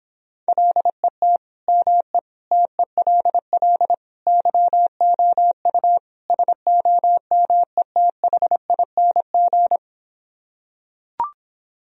Morse code, heard as a keyed tone, about 26 wpm, 700 Hz. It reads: LET ME TELL YOU SOMETHING